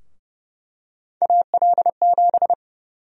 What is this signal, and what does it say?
Morse code, heard as a keyed tone, about 30 wpm, 700 Hz. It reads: AL7
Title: USA Call Sign Prefixes - 30wpm